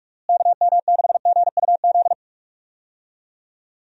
Morse code, heard as a keyed tone, about 45 wpm, 700 Hz. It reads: KM6GUZ
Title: USA Call Signs - 6 Characters - Repeated 2x - 45wpm